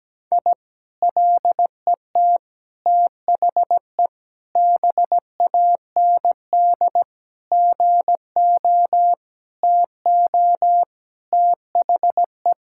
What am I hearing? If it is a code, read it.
Morse code, heard as a keyed tone, about 17 wpm, 700 Hz. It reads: I LET THE BAND GO TO THE
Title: Sentences from Top 1600 Words 17wpm